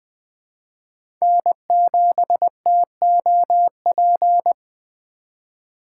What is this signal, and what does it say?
Morse code, heard as a keyed tone, about 20 wpm, 700 Hz. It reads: N7TOP
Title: USA Call Signs - 5 Characters - 3x Character Spacing - 20wpm